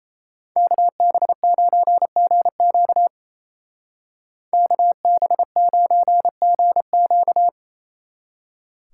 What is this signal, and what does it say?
Morse code, heard as a keyed tone, about 34 wpm, 700 Hz. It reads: KB9GQ KB9GQ
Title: Speed Racing - USA 2x2 Call Signs 28wpm